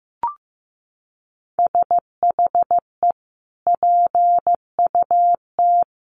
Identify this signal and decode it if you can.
Morse code, heard as a keyed tone, about 15 wpm, 700 Hz. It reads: SHE PUT
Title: Sentences from Top 1400 Words 15wpm